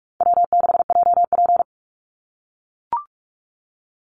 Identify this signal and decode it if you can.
Morse code, heard as a keyed tone, about 44 wpm, 700 Hz. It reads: W6JP
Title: NA SOTA Chaser Callsigns 45wpm